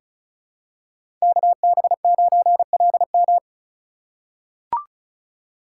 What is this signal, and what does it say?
Morse code, heard as a keyed tone, about 36 wpm, 700 Hz. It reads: KB9LM
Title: USA Call Signs - 2x Character Spacing - 35wpm